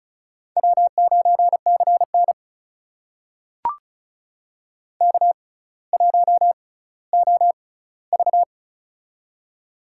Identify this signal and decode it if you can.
Morse code, heard as a keyed tone, about 36 wpm, spaced as for 18 wpm, 700 Hz. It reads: W9CN K1OV
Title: USA Call Signs - 6x Character Spacing - 35wpm